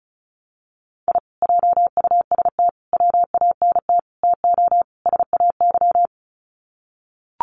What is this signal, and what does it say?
Morse code, heard as a keyed tone, about 34 wpm, 700 Hz. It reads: I JUST WANT TO SAY
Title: Sets of 5 Words Encore 35wpm